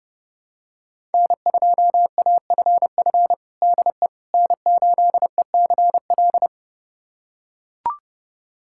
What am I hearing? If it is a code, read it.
Morse code, heard as a keyed tone, about 32 wpm, 700 Hz. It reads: N2AFF DE N8ECL
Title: USA Callsign DE Callsign - 5 Characters - 30wpm